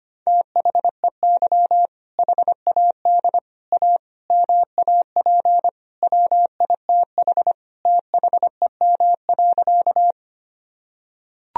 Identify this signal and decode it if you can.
Morse code, heard as a keyed tone, about 25 wpm, 700 Hz. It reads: THEY HAD A MAP WITH THEM.
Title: Sentences from Top 1000 Words - Review Part 2 - 25wpm